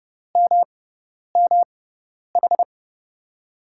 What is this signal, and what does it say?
Morse code, heard as a keyed tone, about 30 wpm, spaced as for 10 wpm, 700 Hz. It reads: MMH
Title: USA Call Signs - 6 Characters - 6x Character Spacing - 30wpm